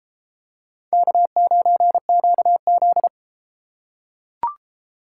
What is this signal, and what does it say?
Morse code, heard as a keyed tone, about 31 wpm, 700 Hz. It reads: K9QZ